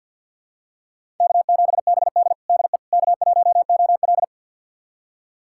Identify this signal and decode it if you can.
Morse code, heard as a keyed tone, about 49 wpm, 700 Hz. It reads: K7BD DE K1OL
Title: USA Callsign DE Callsign - 4 Characters - 50wpm